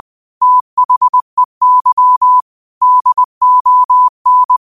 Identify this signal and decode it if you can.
Morse code, heard as a keyed tone, about 20 wpm, 1000 Hz. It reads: THEY DON
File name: qotdfiles / 20wpm